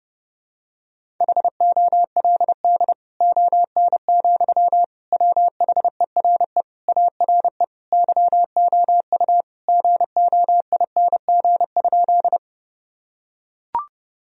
Morse code, heard as a keyed tone, about 31 wpm, 700 Hz. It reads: HOLD ON, WHERE ARE YOU GOING?
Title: Sentences from Top 1000 Words - Review Part 2 - 30wpm